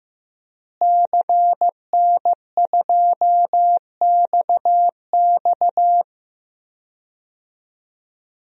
Morse code, heard as a keyed tone, about 15 wpm, 700 Hz. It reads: CN2XX
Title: DXCC-100 Call Signs - 5 Characters - Repeated 3x - 15wpm